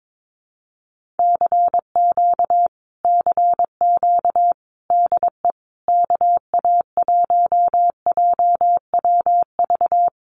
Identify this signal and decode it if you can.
Morse code, heard as a keyed tone, about 22 wpm, 700 Hz. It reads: CQ CQ DE KA1JWV
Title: Common QSO Phrases Encore 22wpm